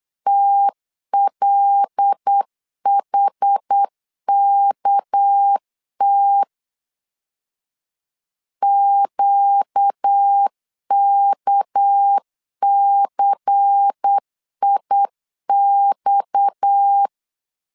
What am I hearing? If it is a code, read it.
Morse code, heard as a keyed tone, about 8 wpm, 800 Hz. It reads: TLHKT QKCIX
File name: LEZIONI / 6 PKX0 / 6 PKX 8wpm 3041